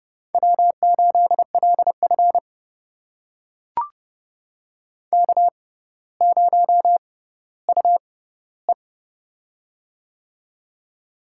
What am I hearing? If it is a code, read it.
Morse code, heard as a keyed tone, about 31 wpm, spaced as for 15 wpm, 700 Hz. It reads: W8LF K0UE
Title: USA Call Signs - 4 Characters - 6x Character Spacing - 30wpm